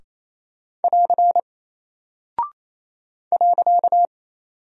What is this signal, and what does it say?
Morse code, heard as a keyed tone, about 28 wpm, 700 Hz. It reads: <AR> .
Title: Single Character (Letter:Number:Punctuation:Prosign) Alt 28wpm